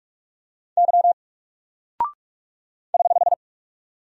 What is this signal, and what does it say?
Morse code, heard as a keyed tone, about 44 wpm, 700 Hz. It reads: Y <HH>
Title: Single Character (Letter:Number:Punctuation:Prosign) 45wpm